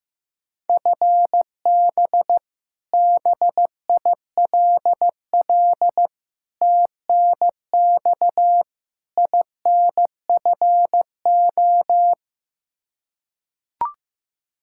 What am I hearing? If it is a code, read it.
Morse code, heard as a keyed tone, about 15 wpm, 700 Hz. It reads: FB BILL TNX INFO
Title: Practice QSO - Exchange 3 - More Info - 15wpm